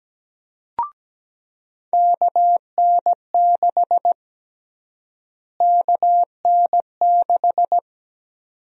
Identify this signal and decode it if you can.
Morse code, heard as a keyed tone, about 17 wpm, 700 Hz. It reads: KN6 KN6
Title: USA Call Sign Prefixes - Repeated 2x - 17wpm